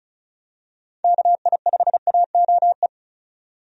Morse code, heard as a keyed tone, about 35 wpm, 700 Hz. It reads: KI5AOE
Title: USA Call Signs - 6 Characters - 35wpm